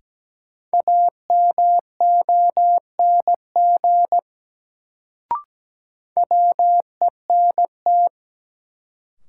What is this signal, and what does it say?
Morse code, heard as a keyed tone, about 17 wpm, 700 Hz. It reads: AMONG WENT
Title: Sets of 2 Words - New Word Drill - 17wpm